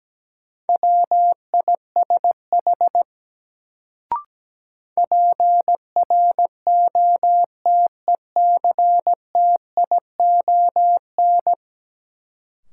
Morse code, heard as a keyed tone, about 17 wpm, 700 Hz. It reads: WISH PROTECTION
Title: Sentences from Top 1600 Words - New Word Drill - 17wpm